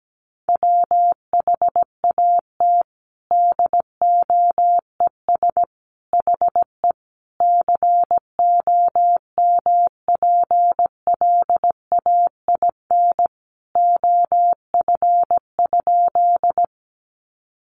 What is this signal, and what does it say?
Morse code, heard as a keyed tone, about 17 wpm, 700 Hz. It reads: WHAT DOES HE COMPLAIN OF?
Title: Sentences from Top 1300 Words 17wpm